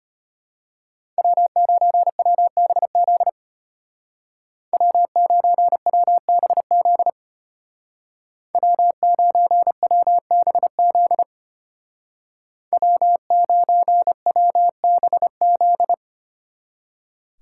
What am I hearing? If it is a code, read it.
Morse code, heard as a keyed tone, about 31 wpm, 700 Hz. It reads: W9WBZ W9WBZ W9WBZ W9WBZ
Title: Speed Racing - USA 1x3 Call Signs 25wpm